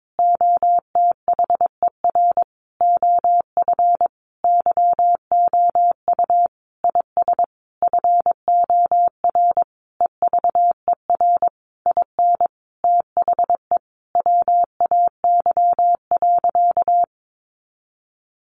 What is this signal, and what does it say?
Morse code, heard as a keyed tone, about 22 wpm, 700 Hz. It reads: OTHER OF YOU IS FOR EVER IN THE WAY.
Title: Sentences from Top 400 Words 22wpm